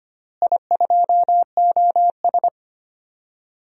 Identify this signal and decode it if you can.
Morse code, heard as a keyed tone, about 25 wpm, 700 Hz. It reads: I2OS
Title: DXCC-100 Call Signs - 4 Characters - Repeated 3x - 25wpm